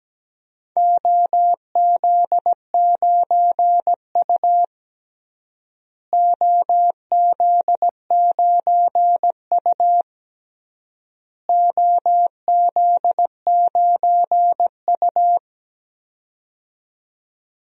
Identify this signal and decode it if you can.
Morse code, heard as a keyed tone, about 17 wpm, 700 Hz. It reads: OZ9U OZ9U OZ9U
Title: DXCC-100 Call Signs - 4 Characters - Repeated 3x - 17wpm